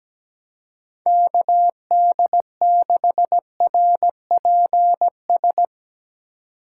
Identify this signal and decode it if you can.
Morse code, heard as a keyed tone, about 17 wpm, 700 Hz. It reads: KD6RPS
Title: USA Call Signs - 6 Characters - 3x Character Spacing - 17wpm